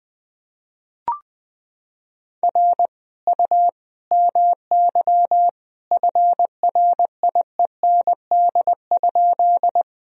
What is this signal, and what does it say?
Morse code, heard as a keyed tone, about 20 wpm, 700 Hz. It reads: R U MY FRIEND?
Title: Sentences from Top 300 Words and Abbreviations 20wpm